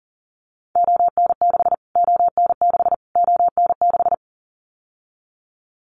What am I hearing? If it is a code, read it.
Morse code, heard as a keyed tone, about 40 wpm, 700 Hz. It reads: ON6 ON6 ON6
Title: Callsign Prefixes for Top 25 DXCC - Repeated 3x - 40wpm